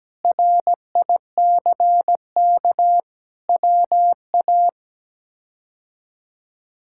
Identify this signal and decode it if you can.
Morse code, heard as a keyed tone, about 17 wpm, 700 Hz. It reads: RICK WA
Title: K1USN SST QSO Practice - 17wpm